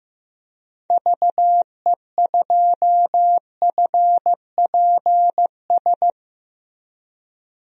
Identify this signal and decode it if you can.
Morse code, heard as a keyed tone, about 15 wpm, 700 Hz. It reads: VE2FPS
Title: DXCC-100 Call Signs - 6 Characters - Repeated 3x - 15wpm